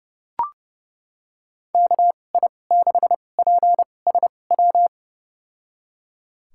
Morse code, heard as a keyed tone, about 30 wpm, spaced as for 22 wpm, 700 Hz. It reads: KI6PSW
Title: USA Call Signs - 6 Characters - 2x Character Spacing - 30wpm